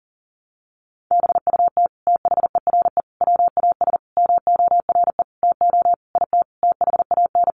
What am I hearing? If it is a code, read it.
Morse code, heard as a keyed tone, about 40 wpm, 700 Hz. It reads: BUT THERE WAS MORE TO IT THAN
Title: Sentences from Top 100 Words 40wpm